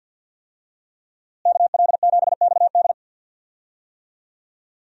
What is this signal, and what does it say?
Morse code, heard as a keyed tone, about 52 wpm, 700 Hz. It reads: KL7XD
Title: DXCC-100 Call Signs - 5 Characters - Repeated 3x - 50wpm